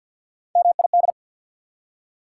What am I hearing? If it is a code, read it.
Morse code, heard as a keyed tone, about 54 wpm, 690 Hz. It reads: MID
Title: Prefixes and Suffixes 50wpm